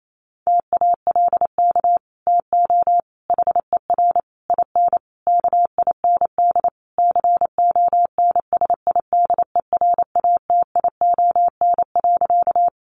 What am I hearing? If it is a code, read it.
Morse code, heard as a keyed tone, about 28 wpm, 700 Hz. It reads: TALK TO HER IN KIND CONSIDERATION.
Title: Sentences from Top 1500 Words 28wpm